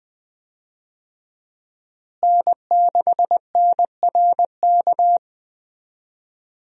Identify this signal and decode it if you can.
Morse code, heard as a keyed tone, about 21 wpm, 700 Hz. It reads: N6NRK
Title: USA Call Signs - 5 Characters - Repeated 2x - 20wpm